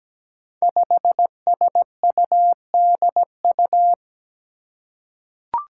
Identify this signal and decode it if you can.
Morse code, heard as a keyed tone, about 17 wpm, 700 Hz. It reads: 5SUDU
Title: Five Challenging Letter-Number Combinations 17wpm